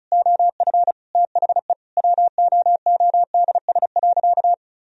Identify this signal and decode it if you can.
Morse code, heard as a keyed tone, about 36 wpm, 700 Hz. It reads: OF THE WOODS.
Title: Sentences from Top 1000 Words - Review Part 2 - 35wpm